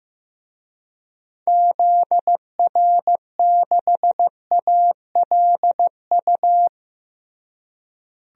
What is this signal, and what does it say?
Morse code, heard as a keyed tone, about 15 wpm, 700 Hz. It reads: ZR6ALU
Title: DXCC-100 Call Signs - 6 Characters - Repeated 3x - 15wpm